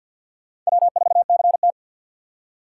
Morse code, heard as a keyed tone, about 60 wpm, 700 Hz. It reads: W4CT
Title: Speed Racing - USA 1x2 Call Signs 50wpm